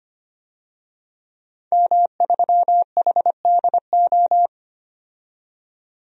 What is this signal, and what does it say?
Morse code, heard as a keyed tone, about 25 wpm, 700 Hz. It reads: M3HDO
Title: DXCC-100 Call Signs - 5 Characters - 25wpm